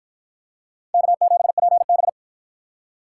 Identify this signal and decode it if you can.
Morse code, heard as a keyed tone, about 48 wpm, 700 Hz. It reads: K7PB